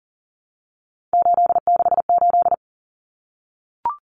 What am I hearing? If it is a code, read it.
Morse code, heard as a keyed tone, about 40 wpm, 700 Hz. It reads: 868
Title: Numbers - 3-Digit 40wpm